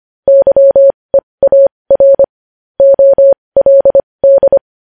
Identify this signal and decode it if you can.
Morse code, heard as a keyed tone, about 25 wpm, spaced as for 20 wpm, 550 Hz. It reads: YEAR OLD
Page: news-headlines-in-morse-code-at-20-wpm